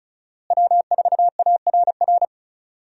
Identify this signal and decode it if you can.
Morse code, heard as a keyed tone, about 36 wpm, 700 Hz. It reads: W4ARR
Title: USA Call Signs - 5 Characters - Repeated 3x - 35wpm